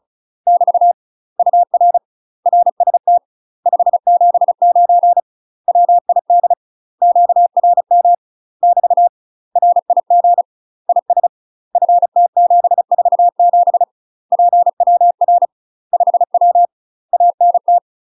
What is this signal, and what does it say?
Morse code, heard as a keyed tone, about 36 wpm, 700 Hz. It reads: <BT> UR RST 579 WID QRM <BT> RIG IS FT747 PWR 5W ANT
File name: UK MASTER QSOs TRAINING / 35WPM / UK MASTER QSOs 0013